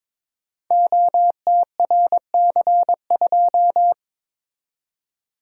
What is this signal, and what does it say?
Morse code, heard as a keyed tone, about 22 wpm, 700 Hz. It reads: OTRC2
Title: Code Groups - 5 Letter:Numbers - 22wpm